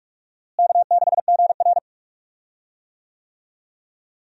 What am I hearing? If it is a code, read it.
Morse code, heard as a keyed tone, about 46 wpm, 700 Hz. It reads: K6GR
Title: USA Call Signs - 4 Characters - Repeated 3x - 45wpm